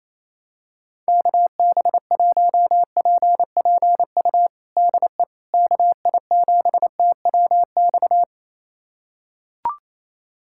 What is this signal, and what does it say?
Morse code, heard as a keyed tone, about 28 wpm, 700 Hz. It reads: KB1PPU DE KI7TWX
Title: USA Callsign DE Callsign - 6 Characters - 28wpm